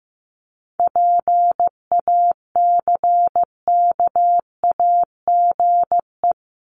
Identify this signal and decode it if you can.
Morse code, heard as a keyed tone, about 15 wpm, 700 Hz. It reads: PACKAGE
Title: Top 1501-1750 Words 15wpm